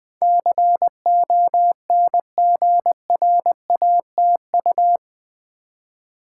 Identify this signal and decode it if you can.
Morse code, heard as a keyed tone, about 20 wpm, 700 Hz. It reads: CONGRATU
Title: Build A Word - 4 Letters At A Time - 12-Letter Words - 20wpm